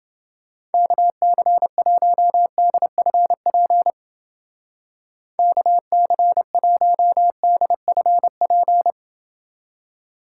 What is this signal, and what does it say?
Morse code, heard as a keyed tone, about 28 wpm, 700 Hz. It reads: KC1DFP KC1DFP